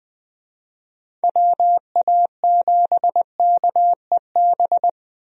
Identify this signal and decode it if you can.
Morse code, heard as a keyed tone, about 20 wpm, 700 Hz. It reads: WA7KEB
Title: USA Call Signs - 6 Characters - Repeated 2x - 20wpm